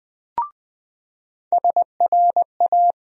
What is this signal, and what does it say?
Morse code, heard as a keyed tone, about 20 wpm, 700 Hz. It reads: SRA
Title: USA Call Sign Postfixes - 3 Letters 20wpm